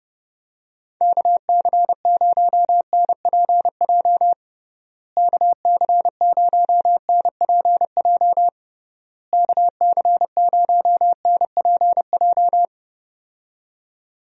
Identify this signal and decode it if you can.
Morse code, heard as a keyed tone, about 30 wpm, 700 Hz. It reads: KC0NPJ KC0NPJ KC0NPJ
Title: USA Call Signs - 6 Characters - Repeated 3x - 30wpm